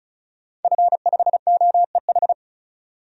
Morse code, heard as a keyed tone, about 36 wpm, 700 Hz. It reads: F5OEH
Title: DXCC-100 Call Signs - 5 Characters - 35wpm